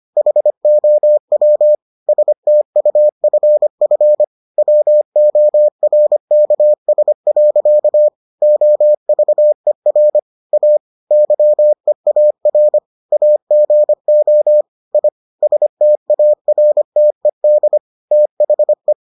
Morse code, heard as a keyed tone, about 25 wpm, 600 Hz. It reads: HOW STUFF WORKS. OVER A YEAR AGO I STARTED THE